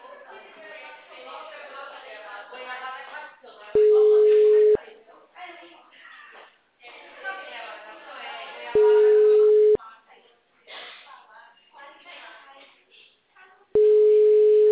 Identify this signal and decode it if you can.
Ringback tone (Europe, 425 Hz), with speech.
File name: out- -2005- - - .88531